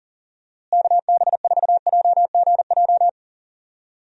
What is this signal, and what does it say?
Morse code, heard as a keyed tone, about 40 wpm, 700 Hz. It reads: KB4JGJ
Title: USA Call Signs - 6 Characters - Repeated 3x - 40wpm